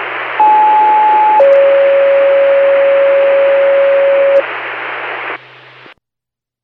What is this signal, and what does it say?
Two-tone page, heard as a one-second tone, then a three-second tone, about 860 then 550 Hz, with noise. It is heard from 0.4 to 4.4 s.